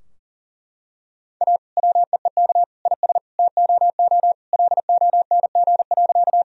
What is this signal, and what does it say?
Morse code, heard as a keyed tone, about 40 wpm, 700 Hz. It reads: A WEEK IS TOO LONG.
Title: Sentences from Top 1000 Words - Review Part 2 - 40wpm